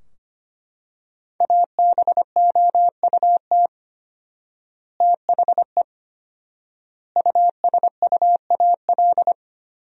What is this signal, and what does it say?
Morse code, heard as a keyed tone, about 25 wpm, 700 Hz. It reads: ABOUT THE USUAL
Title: 4x Word-Spacing - Sentences from Top 400 Words - 25wpm